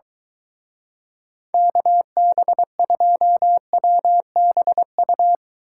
Morse code, heard as a keyed tone, about 23 wpm, 700 Hz. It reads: KB2WBU